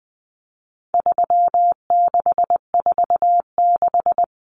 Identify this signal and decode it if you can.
Morse code, heard as a keyed tone, about 20 wpm, 700 Hz. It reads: 3646
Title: Common Cut Numbers - 4-Digit 20wpm